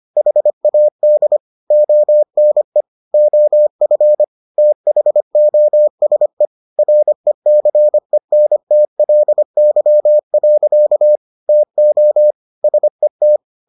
Morse code, heard as a keyed tone, about 25 wpm, 600 Hz. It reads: HAD ONE OF THOSE RECENTLY. TO SET